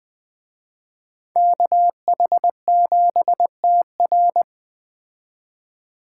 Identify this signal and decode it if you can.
Morse code, heard as a keyed tone, about 21 wpm, 700 Hz. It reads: KH7TR